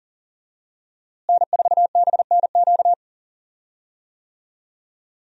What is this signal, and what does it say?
Morse code, heard as a keyed tone, about 40 wpm, 700 Hz. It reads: N4BNQ
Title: USA Call Signs - 5 Characters - Repeated 2x - 40wpm